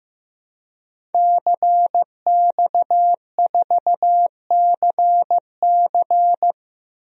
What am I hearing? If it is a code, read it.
Morse code, heard as a keyed tone, about 16 wpm, 700 Hz. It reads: CX4CC
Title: DXCC-100 Call Signs - 5 Characters - Repeated 2x - 15wpm